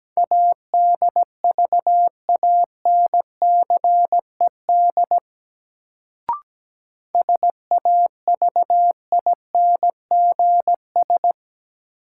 Morse code, heard as a keyed tone, about 17 wpm, 700 Hz. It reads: ADVANCED SAVINGS
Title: Sentences from Top 1200 Words - New Word Drill - 17wpm